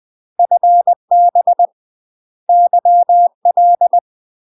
Morse code, heard as a keyed tone, about 20 wpm, 700 Hz. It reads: FB YL
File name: UK MASTER QSOs TRAINING / 20WPM / UK MASTER QSOs 0062